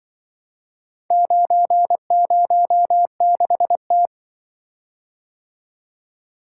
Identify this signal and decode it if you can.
Morse code, heard as a keyed tone, about 25 wpm, 700 Hz. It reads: 906T
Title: Speed-Racing - Four Letter-Number Combinations - 20wpm